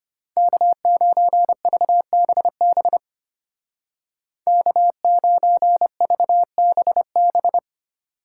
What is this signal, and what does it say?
Morse code, heard as a keyed tone, about 31 wpm, 700 Hz. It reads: K9VBB K9VBB
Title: Speed Racing - USA 1x3 Call Signs 25wpm